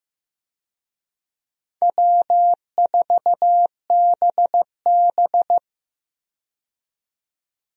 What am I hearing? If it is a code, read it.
Morse code, heard as a keyed tone, about 15 wpm, 700 Hz. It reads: W4BB
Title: USA Call Signs - 4 Characters - 15wpm